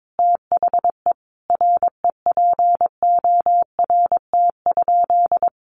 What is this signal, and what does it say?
Morse code, heard as a keyed tone, about 22 wpm, 700 Hz. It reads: THE REPORT?
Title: Sentences from Top 1200 Words 22wpm